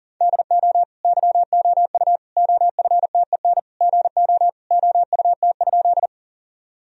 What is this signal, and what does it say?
Morse code, heard as a keyed tone, about 40 wpm, 700 Hz. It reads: DO YOU OFTEN GO OUT?
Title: Sentences from Top 1000 Words - Review Part 1 - 40wpm